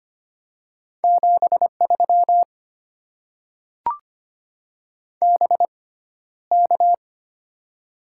Morse code, heard as a keyed tone, about 25 wpm, spaced as for 13 wpm, 700 Hz. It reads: 73 BK
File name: Top 90 QSO Elements - 6x Character Spacing - 25wpm